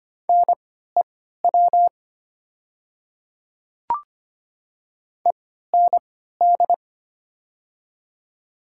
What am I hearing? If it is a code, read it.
Morse code, heard as a keyed tone, about 25 wpm, spaced as for 14 wpm, 700 Hz. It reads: NEW END
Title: 3x Character Spacing - Top 200 Words - 25wpm